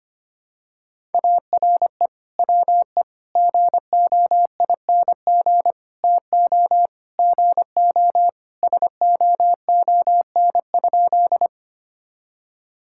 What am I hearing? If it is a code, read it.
Morse code, heard as a keyed tone, about 25 wpm, 700 Hz. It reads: ARE WE GOING TO GO SOON?
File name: practice / 25wpm / Sentences with 3-Word Phrases - 25wpm